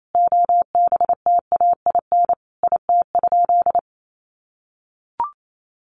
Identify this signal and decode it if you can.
Morse code, heard as a keyed tone, about 28 wpm, 700 Hz. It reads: OBTAIN IT?
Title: Sentences from Top 1300 Words 28wpm